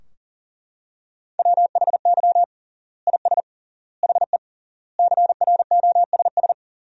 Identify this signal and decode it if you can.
Morse code, heard as a keyed tone, about 37 wpm, 700 Hz. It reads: WHY IS HE CROSS
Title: 3x Word-Spacing - Sentences from Top 300 Words - 40wpm